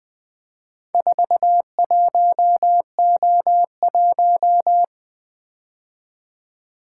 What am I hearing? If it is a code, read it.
Morse code, heard as a keyed tone, about 20 wpm, 700 Hz. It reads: 41O1
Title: Four Letter-Number Combinations - 20wpm